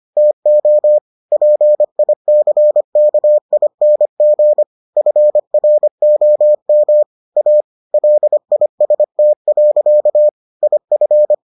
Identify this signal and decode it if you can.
Morse code, heard as a keyed tone, about 25 wpm, 600 Hz. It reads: TO PICKING FROM A LIST. IF